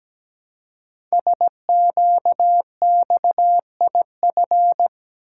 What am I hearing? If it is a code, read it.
Morse code, heard as a keyed tone, about 17 wpm, 700 Hz. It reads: SQXIF
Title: Code Groups - 5 Letter:Numbers - 17wpm